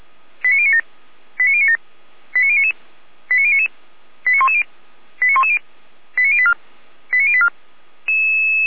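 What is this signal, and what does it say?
Five-tone selective call (ZVEI), heard as quick runs of five tones, tones of about 80 ms. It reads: 89097 89097 89099 89099 89110 89110 89084 89084
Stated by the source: ZVEI five-tone selective call received by SDR.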